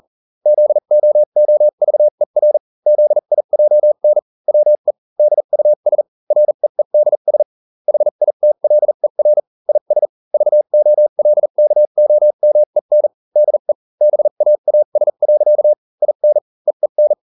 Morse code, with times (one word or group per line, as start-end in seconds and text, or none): 0.5-2.6 ZOOVER
2.9-4.2 ZIJN
4.5-4.9 WE
5.2-6.0 DUS
6.3-7.4 REEDS
7.9-9.4 HITLER
9.7-10.1 IS
10.3-13.1 VOLKOMEN
13.4-13.7 DE
14.0-15.7 BAAS.
16.0-16.4 IN
16.7-17.1 EEN